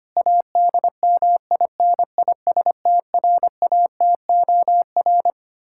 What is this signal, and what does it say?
Morse code, heard as a keyed tone, about 25 wpm, 700 Hz. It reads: ADMINISTRATOR
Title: K1USN SST QSO Practice 25wpm